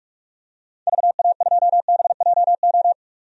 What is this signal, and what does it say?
Morse code, heard as a keyed tone, about 41 wpm, 700 Hz. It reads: VA2BJO
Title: DXCC-100 Call Signs - 6 Characters - 45wpm